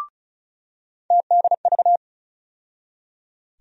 Morse code, heard as a keyed tone, about 34 wpm, 700 Hz. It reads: TDV
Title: USA Call Sign Postfixes - 3 Letters 35wpm